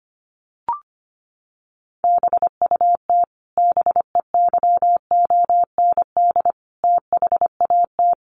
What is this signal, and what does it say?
Morse code, heard as a keyed tone, about 25 wpm, 700 Hz. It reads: BUT BEYOND THAT
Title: Sentences from Top 1300 Words 25wpm